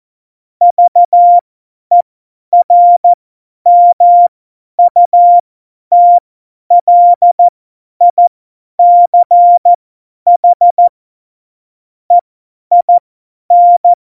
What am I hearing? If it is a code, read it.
Morse code, heard as a keyed tone, about 14 wpm, spaced as for 10 wpm, 700 Hz. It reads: VERMUTLICH EIN
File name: Morsekurs / Witze / Witz 11. Juni 2019, 50 F70